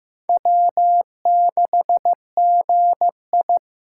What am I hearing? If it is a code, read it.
Morse code, heard as a keyed tone, about 18 wpm, 700 Hz. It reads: W6GI
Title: Speed Racing - USA 1x2 Call Signs 15wpm